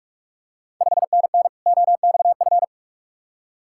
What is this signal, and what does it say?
Morse code, heard as a keyed tone, about 45 wpm, 700 Hz. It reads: 5NN OXF
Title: Ontario QSO Party Exchange 45wpm